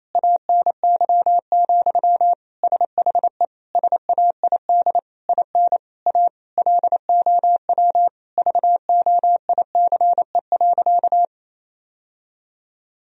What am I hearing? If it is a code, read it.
Morse code, heard as a keyed tone, about 28 wpm, 700 Hz. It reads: ANY, SHE SAID IN A LOW VOICE.
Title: Sentences from Top 1000 Words - Review Part 1 - 28wpm